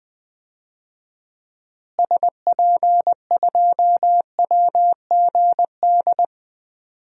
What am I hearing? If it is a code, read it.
Morse code, heard as a keyed tone, about 21 wpm, 700 Hz. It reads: SP2WGD
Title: DXCC-100 Call Signs - 6 Characters - Repeated 2x - 20wpm